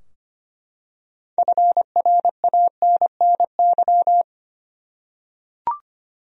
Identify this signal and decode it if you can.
Morse code, heard as a keyed tone, about 25 wpm, 700 Hz. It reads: FRANNY
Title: USA Common Names - 25wpm